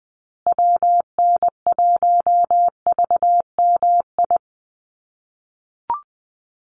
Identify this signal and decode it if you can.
Morse code, heard as a keyed tone, about 20 wpm, 700 Hz. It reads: WN1VMI
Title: USA Call Signs Repeated 3x 20wpm